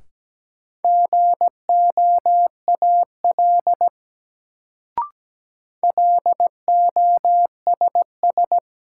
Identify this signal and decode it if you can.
Morse code, heard as a keyed tone, about 17 wpm, 700 Hz. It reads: GOAL LOSS
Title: Sentences from Top 1400 Words - New Word Drill Alt - 17wpm